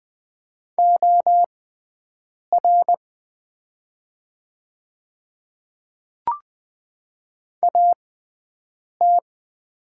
Morse code, heard as a keyed tone, about 20 wpm, spaced as for 7 wpm, 700 Hz. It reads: OR AT
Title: 6x Character Spacing - Top 100 Words - 20wpm